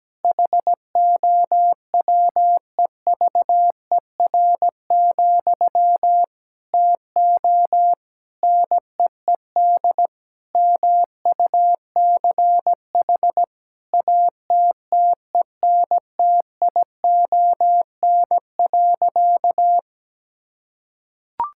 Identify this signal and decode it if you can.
Morse code, heard as a keyed tone, about 17 wpm, 700 Hz. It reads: HOWEVER, TO NEED MUCH ATTENTION.
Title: Sentences from Top 1000 Words - Review Part 6 - 17wpm